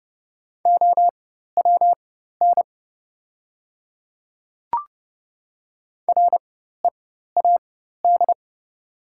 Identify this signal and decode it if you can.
Morse code, heard as a keyed tone, about 30 wpm, spaced as for 14 wpm, 700 Hz. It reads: OWN READ
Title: 4x Character Spacing - Top 300 Words - 30wpm